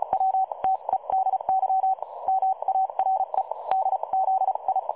Morse code, heard as a keyed tone, about 33 wpm, spaced as for 27 wpm, 760 Hz. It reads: WTEDY MAREDZI
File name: dlazaawansowanych / nagraniazpasma / websdr recording 3565.0kHz